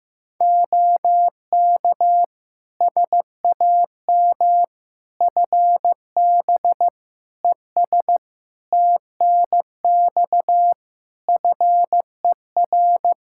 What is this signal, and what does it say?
Morse code, heard as a keyed tone, about 15 wpm, 700 Hz. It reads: OK SAM FB ES TNX FER
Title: Practice QSO - Exchange 2 - Info - 15wpm